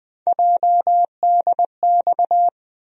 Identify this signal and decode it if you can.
Morse code, heard as a keyed tone, about 20 wpm, 700 Hz. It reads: JDX
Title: USA Call Sign Postfixes - 3 Letters 20wpm